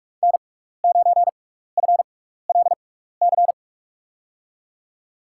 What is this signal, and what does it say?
Morse code, heard as a keyed tone, about 41 wpm, spaced as for 15 wpm, 700 Hz. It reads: N9FLC
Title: USA Call Signs - 5 Characters - 6x Character Spacing - 45wpm